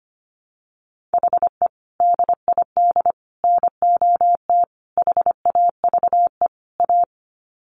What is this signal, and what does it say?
Morse code, heard as a keyed tone, about 25 wpm, 700 Hz. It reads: HE DID NOT HAVE A
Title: Sets of 5 Words Encore 25wpm